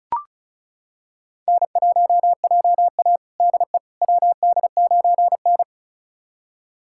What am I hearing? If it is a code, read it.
Morse code, heard as a keyed tone, about 36 wpm, 700 Hz. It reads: N1JA DE WD9N
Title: USA Callsign DE Callsign - 4 Characters - 35wpm